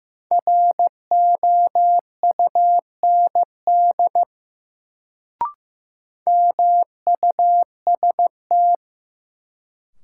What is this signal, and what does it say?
Morse code, heard as a keyed tone, about 15 wpm, 700 Hz. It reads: ROUND MUST
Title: Sentences from Top 200 Words - New Word Drill - 15wpm